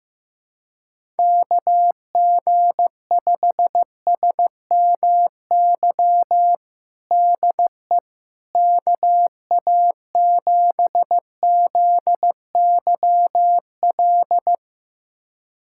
Morse code, heard as a keyed tone, about 15 wpm, 700 Hz. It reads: KG5SMY DE KA7ZYL
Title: USA Callsign DE Callsign - 6 Characters - 15wpm